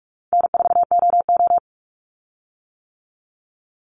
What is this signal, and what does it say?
Morse code, heard as a keyed tone, about 44 wpm, 700 Hz. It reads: N4OO
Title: Call Signs - 45wpm